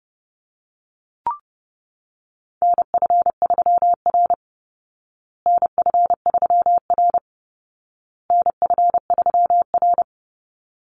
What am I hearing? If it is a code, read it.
Morse code, heard as a keyed tone, about 30 wpm, 700 Hz. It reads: NF3R NF3R NF3R
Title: Call Signs Repeated 2x - 30wpm